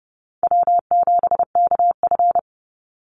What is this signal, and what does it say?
Morse code, heard as a keyed tone, about 30 wpm, 700 Hz. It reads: W7KF
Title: Easy Call Signs Repeated 2x - 30wpm